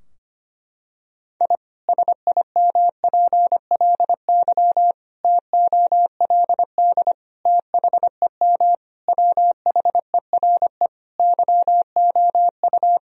Morse code, heard as a keyed tone, about 25 wpm, 700 Hz. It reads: I SIMPLY TOLD THEM WHERE YOU
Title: Sentences from Top 1000 Words - Review Part 4 - 25wpm